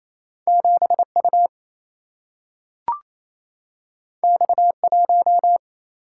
Morse code, heard as a keyed tone, about 28 wpm, 700 Hz. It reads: 7U X1
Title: Single-Letter Single-Number 28wpm